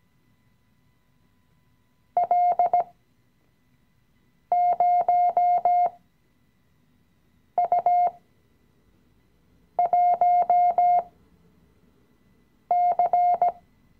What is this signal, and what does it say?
Morse code, heard as a keyed tone, about 17 wpm, spaced as for 5 wpm, 710 Hz. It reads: L0U1C
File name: CW-Online-Kurs / Anfänger / Lektion 017